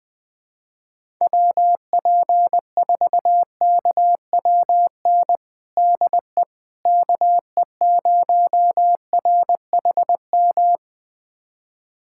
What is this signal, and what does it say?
Morse code, heard as a keyed tone, about 20 wpm, 700 Hz. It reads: WP4KWN DE KE0RHM
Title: USA Callsign DE Callsign - 6 Characters - 20wpm